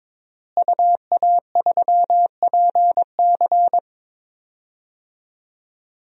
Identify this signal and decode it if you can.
Morse code, heard as a keyed tone, about 21 wpm, 700 Hz. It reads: UA3PC
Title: DXCC-100 Call Signs - 5 Characters - Repeated 3x - 22wpm